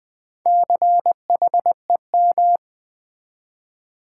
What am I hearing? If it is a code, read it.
Morse code, heard as a keyed tone, about 20 wpm, 700 Hz. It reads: CHEM
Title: Build A Word - 4 Letters At A Time - 8-Letter Words - 20wpm